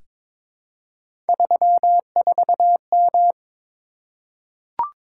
Morse code, heard as a keyed tone, about 22 wpm, 700 Hz. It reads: 34M
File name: practice / 22wpm / Three Letter-Number Combinations - 22wpm